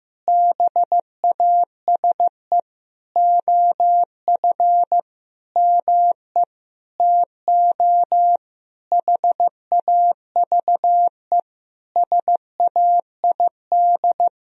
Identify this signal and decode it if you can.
Morse code, heard as a keyed tone, about 15 wpm, 700 Hz. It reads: BASE OF ME TO HAVE SAID
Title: Sentences from Top 1000 Words - Review Part 1 - 15wpm